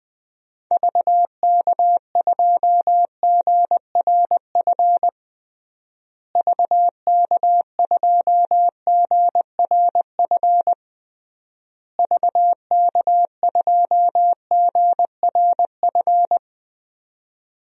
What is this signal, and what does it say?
Morse code, heard as a keyed tone, about 20 wpm, 700 Hz. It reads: VK2GRF VK2GRF VK2GRF
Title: DXCC-100 Call Signs - 6 Characters - Repeated 3x - 20wpm